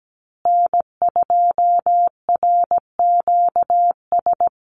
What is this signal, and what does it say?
Morse code, heard as a keyed tone, about 17 wpm, 700 Hz. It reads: N2RQS
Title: USA Call Signs Repeated 2x 17wpm